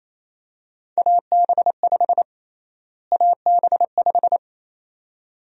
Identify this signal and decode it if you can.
Morse code, heard as a keyed tone, about 28 wpm, 700 Hz. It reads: AB5 AB5
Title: USA Call Sign Prefixes - Repeated 2x - 28wpm